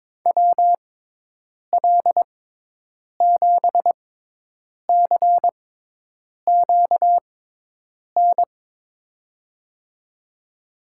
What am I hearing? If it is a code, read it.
Morse code, heard as a keyed tone, about 22 wpm, spaced as for 8 wpm, 700 Hz. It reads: WL7CQN
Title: USA Call Signs - 6 Characters - 6x Character Spacing - 22wpm